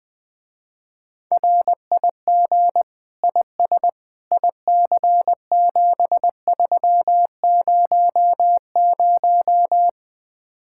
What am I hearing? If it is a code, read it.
Morse code, heard as a keyed tone, about 20 wpm, 700 Hz. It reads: RIG IS IC7300
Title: Speed-Racing Common QSO Phrases - 17wpm